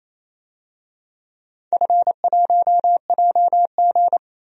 Morse code, heard as a keyed tone, about 27 wpm, 700 Hz. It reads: F1JG
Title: DXCC-100 Call Signs - 4 Characters - 28wpm